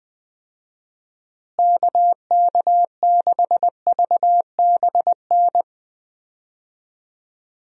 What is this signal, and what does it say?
Morse code, heard as a keyed tone, about 20 wpm, 700 Hz. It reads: KK6VBN
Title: USA Call Signs - 6 Characters - Repeated 3x - 20wpm